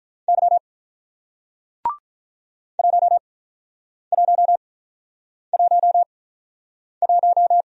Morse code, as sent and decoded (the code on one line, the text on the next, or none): -.-- / .---- .---- .---- .----
Y 1111